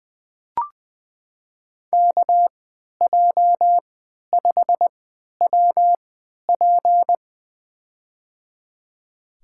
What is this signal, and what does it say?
Morse code, heard as a keyed tone, about 20 wpm, spaced as for 11 wpm, 700 Hz. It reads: KJ5WP
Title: USA Call Signs - 5 Characters - 3x Character Spacing - 20wpm